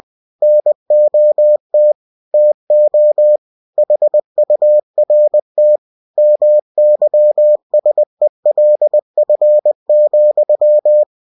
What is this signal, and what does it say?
Morse code, heard as a keyed tone, about 20 wpm, 600 Hz. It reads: NOT TO HURT MYSELF,
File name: cw-books / winnie-the-pooh / winnie-20wpm-page-0036